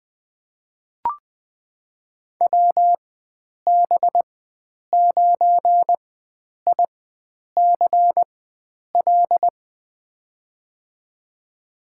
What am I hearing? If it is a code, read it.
Morse code, heard as a keyed tone, about 20 wpm, spaced as for 9 wpm, 700 Hz. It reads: WB9ICL
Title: USA Call Signs - 6 Characters - 4x Character Spacing - 20wpm